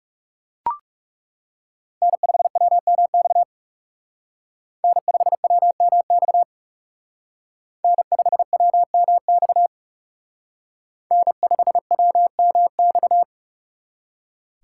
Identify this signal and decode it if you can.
Morse code, heard as a keyed tone, about 39 wpm, 700 Hz. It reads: N5WMX N5WMX N5WMX N5WMX